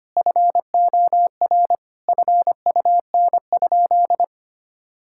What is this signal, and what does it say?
Morse code, heard as a keyed tone, about 25 wpm, 700 Hz. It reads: FOR FUN?
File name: practice / 25wpm / Sentences with 4-Word Phrases - 25wpm